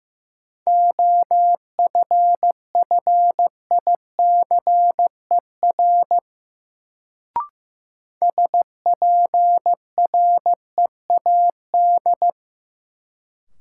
Morse code, heard as a keyed tone, about 15 wpm, 700 Hz. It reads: OFFICER SPREAD
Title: Sentences from Top 1200 Words - New Word Drill - 15wpm